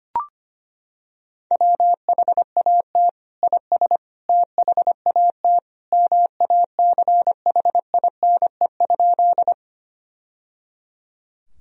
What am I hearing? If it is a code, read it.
Morse code, heard as a keyed tone, about 25 wpm, 700 Hz. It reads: WHAT IS THAT MACHINE?
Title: Sentences from Top 1000 Words - Review Part 2 - 25wpm